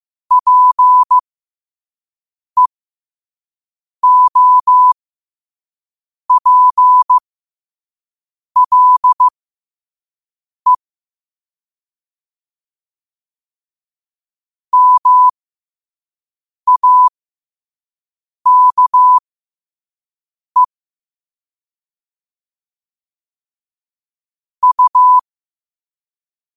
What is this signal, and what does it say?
Morse code, heard as a keyed tone, about 15 wpm, spaced as for 5 wpm, 1000 Hz. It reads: PEOPLE MAKE U
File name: qotdfiles / 5wpm